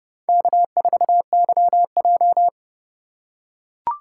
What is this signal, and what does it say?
Morse code, heard as a keyed tone, about 30 wpm, 700 Hz. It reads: K4YJ
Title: USA Call Signs - 4 Characters - 2x Character Spacing - 30wpm